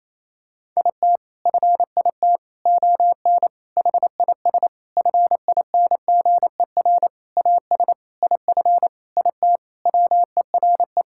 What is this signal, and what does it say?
Morse code, heard as a keyed tone, about 28 wpm, 700 Hz. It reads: IT FIT ON HIS FINGER AS IF IT WERE
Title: Sentences from Top 1600 Words 28wpm